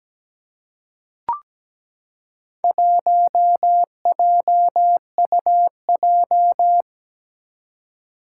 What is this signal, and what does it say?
Morse code, heard as a keyed tone, about 17 wpm, 700 Hz. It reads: 1JUJ
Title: Four Challenging Letter-Number Combinations 17wpm